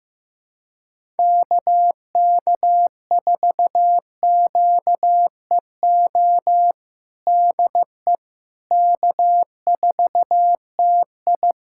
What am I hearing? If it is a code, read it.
Morse code, heard as a keyed tone, about 15 wpm, 700 Hz. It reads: KK4QEO DE K4TI
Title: USA Callsign DE Callsign 15wpm